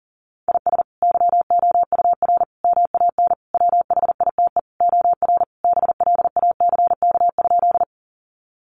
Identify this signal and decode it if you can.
Morse code, heard as a keyed tone, about 40 wpm, 700 Hz. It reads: IS YOUR MAN WHITE OR BLACK?
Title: Sentences from Top 400 Words 40wpm